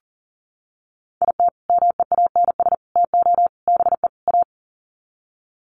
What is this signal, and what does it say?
Morse code, heard as a keyed tone, about 38 wpm, 700 Hz. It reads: IT MEANS TO BE A
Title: Sets of 5 Words Encore2 40wpm